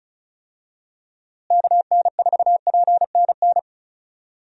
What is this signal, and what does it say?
Morse code, heard as a keyed tone, about 36 wpm, 700 Hz. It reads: KN4PNN
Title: USA Call Signs - 6 Characters - Repeated 3x - 35wpm